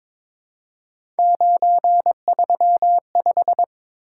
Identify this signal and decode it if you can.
Morse code, heard as a keyed tone, about 21 wpm, 700 Hz. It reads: 935
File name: practice / 22wpm / Three Letter-Number Combinations - 22wpm